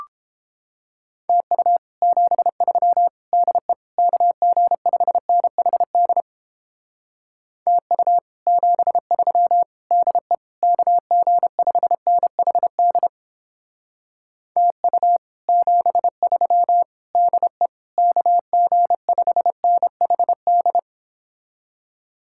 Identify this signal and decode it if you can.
Morse code, heard as a keyed tone, about 31 wpm, 700 Hz. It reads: TU 73 DE KG5NHD TU 73 DE KG5NHD TU 73 DE KG5NHD